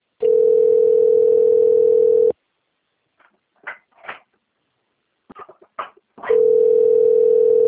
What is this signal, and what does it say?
Ringback tone (North America), 440 + 480 Hz.
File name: rg-1600-unknown- - - .20387